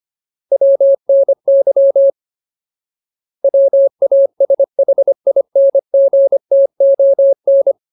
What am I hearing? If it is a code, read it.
Morse code, heard as a keyed tone, about 25 wpm, 550 Hz. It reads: WNY WASHINGTON
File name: Ultimate Word List / 25WPM Ultimate